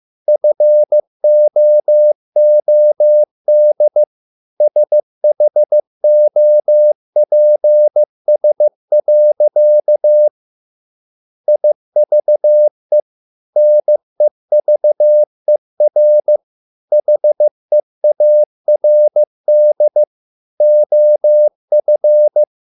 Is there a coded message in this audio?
Yes: Morse code from 0.3 to 22.4 s, heard as a keyed tone, about 15 wpm, 600 Hz. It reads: FOOD SHOPS. IVE NEVER HEARD OF